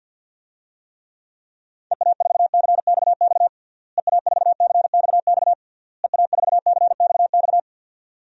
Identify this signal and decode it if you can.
Morse code, heard as a keyed tone, about 52 wpm, 700 Hz. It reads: EA4CXX EA4CXX EA4CXX
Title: DXCC-100 Call Signs - 6 Characters - Repeated 3x - 50wpm